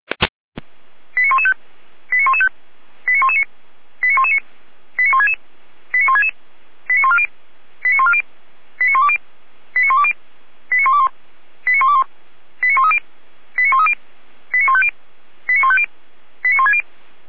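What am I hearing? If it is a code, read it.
Five-tone selective call (ZVEI), heard as quick runs of five tones, tones of about 70 ms. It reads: 89115 89115 89119 89119 89166 89166 89140 89140 89120 89120 89121 89121 89130 89130 89150 89150 89160
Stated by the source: Five-tone ZVEI selective call received by SDR.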